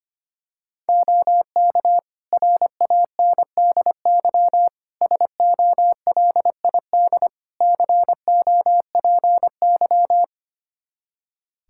Morse code, heard as a keyed tone, about 25 wpm, 700 Hz. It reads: OK RANDY SOLID COPY
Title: Practice QSO - Exchange 3 - More Info - 25wpm